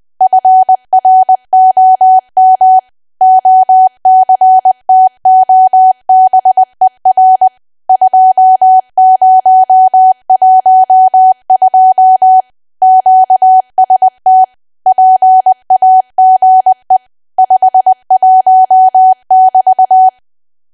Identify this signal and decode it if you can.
Morse code, heard as a keyed tone, about 20 wpm, 750 Hz. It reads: FROM OCTOBER 2012 QST PAGE 51 <BT>
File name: Archive / 20 WPM / 20WPM